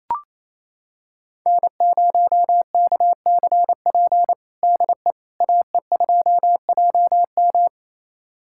Morse code, heard as a keyed tone, about 28 wpm, 700 Hz. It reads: N0KCP DE AE2JM
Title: USA Callsign DE Callsign 28wpm